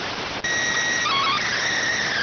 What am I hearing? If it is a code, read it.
SSTV (slow-scan television), PD 290.